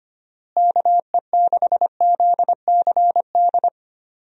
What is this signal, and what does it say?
Morse code, heard as a keyed tone, about 25 wpm, 700 Hz. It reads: KE6ZCD
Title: USA Call Signs - 6 Characters - Repeated 2x - 25wpm